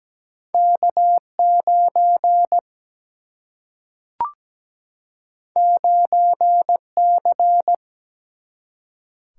Morse code, heard as a keyed tone, about 17 wpm, 700 Hz. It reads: K9 9C
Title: Single-Letter Single-Number 17wpm